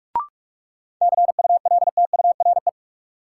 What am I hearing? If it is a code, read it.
Morse code, heard as a keyed tone, about 46 wpm, 700 Hz. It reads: CULTURE
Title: Sentences from Top 1000 Words - Review Part 6 - New Word Drill - 45wpm